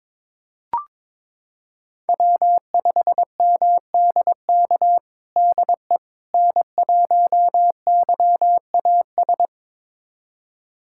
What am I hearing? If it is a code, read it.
Morse code, heard as a keyed tone, about 22 wpm, 700 Hz. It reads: W5MDK DE N1YAS
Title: USA Callsign DE Callsign - 5 Characters - 22wpm